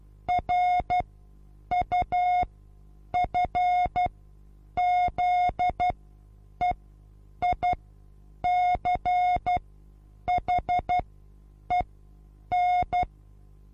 Morse code, heard as a keyed tone, about 12 wpm, spaced as for 8 wpm, 720 Hz. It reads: RUFZEICHEN
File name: Onlinelehrgang / morsen / DJ4UF Morsen23